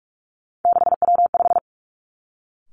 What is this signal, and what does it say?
Morse code, heard as a keyed tone, about 46 wpm, 700 Hz. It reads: 6W5
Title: Three Challenging Letter-Number Combinations 45wpm